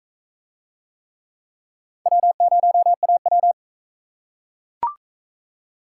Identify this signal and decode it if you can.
Morse code, heard as a keyed tone, about 42 wpm, 700 Hz. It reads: W0AW